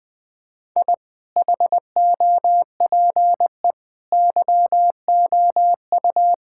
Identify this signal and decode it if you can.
Morse code, heard as a keyed tone, about 21 wpm, 700 Hz. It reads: I HOPE YOU
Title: Sentences with 8-Word Phrases - 20wpm